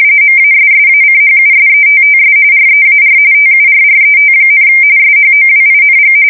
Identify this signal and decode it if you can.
RTTY, about 45.45 baud. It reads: MASSACHUSETTS REAR TURRET, SHOWING
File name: fsk170 / USS-Massachusetts